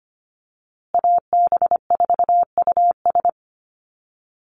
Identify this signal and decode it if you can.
Morse code, heard as a keyed tone, about 25 wpm, 700 Hz. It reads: AB4US
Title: USA Call Signs Repeated 2x 25wpm